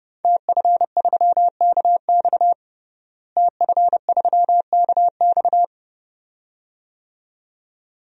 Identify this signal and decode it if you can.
Morse code, heard as a keyed tone, about 30 wpm, 700 Hz. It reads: TF3KX TF3KX
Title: DXCC-100 Call Signs - 5 Characters - Repeated 2x - 30wpm